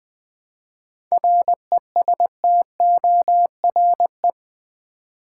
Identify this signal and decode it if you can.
Morse code, heard as a keyed tone, about 20 wpm, 700 Hz. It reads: RESTORE
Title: 7 Letter Words - First 3 Letters Given - 20wpm